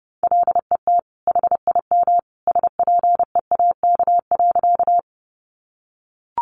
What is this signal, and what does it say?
Morse code, heard as a keyed tone, about 30 wpm, 700 Hz. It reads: LET HIM SPEAK.
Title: Sentences from Top 1100 Words 30wpm